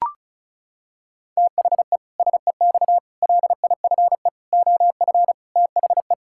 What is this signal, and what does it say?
Morse code, heard as a keyed tone, about 36 wpm, 700 Hz. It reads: THE SEX LIFE OF THE
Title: Pangram Challenge v2 - 35wpm